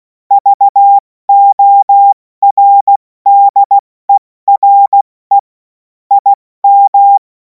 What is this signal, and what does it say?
Morse code, heard as a keyed tone, about 16 wpm, 800 Hz. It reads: VORDERE IM
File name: Morsekurs / Witze / Witz 17. Februar 2026 70 F80